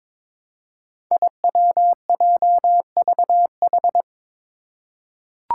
Morse code, heard as a keyed tone, about 22 wpm, 700 Hz. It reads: IWJVH
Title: Five Challenging Letter-Number Combinations 22wpm